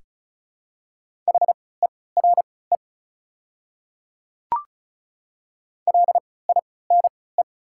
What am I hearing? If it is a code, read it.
Morse code, heard as a keyed tone, about 36 wpm, spaced as for 20 wpm, 700 Hz. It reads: HERE LINE
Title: 3x Character Spacing - Top 300 Words - 35wpm